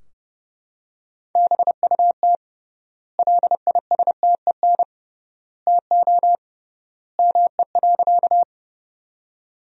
Morse code, heard as a keyed tone, about 32 wpm, 700 Hz. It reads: BUT LISTEN TO ME.
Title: 3x Word-Spacing - Sentences from Top 400 Words - 30wpm